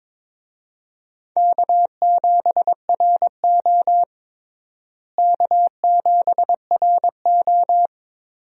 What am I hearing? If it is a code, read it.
Morse code, heard as a keyed tone, about 22 wpm, 700 Hz. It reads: K7RO K7RO
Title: USA Call Signs - 4 Characters - Repeated 2x - 22wpm